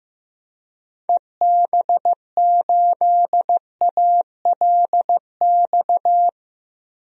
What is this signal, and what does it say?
Morse code, heard as a keyed tone, about 15 wpm, 700 Hz. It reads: EB8ALX
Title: DXCC-100 Call Signs - 6 Characters - 15wpm